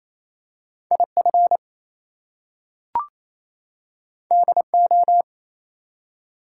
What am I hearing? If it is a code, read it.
Morse code, heard as a keyed tone, about 27 wpm, 700 Hz. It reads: IF DO
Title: Two Letter Words - 28wpm